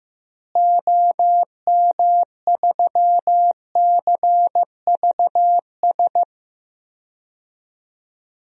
Morse code, heard as a keyed tone, about 15 wpm, 700 Hz. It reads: OM3CVS
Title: DXCC-100 Call Signs - 6 Characters - Repeated 3x - 15wpm